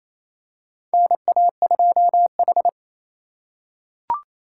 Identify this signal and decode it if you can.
Morse code, heard as a keyed tone, about 28 wpm, 700 Hz. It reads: NA2H
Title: USA Call Signs - 4 Characters - 4x Character Spacing - 28wpm